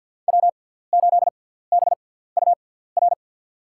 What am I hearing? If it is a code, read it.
Morse code, heard as a keyed tone, about 52 wpm, spaced as for 17 wpm, 700 Hz. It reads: W8BUR
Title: USA Call Signs - 5 Characters - 6x Character Spacing - 50wpm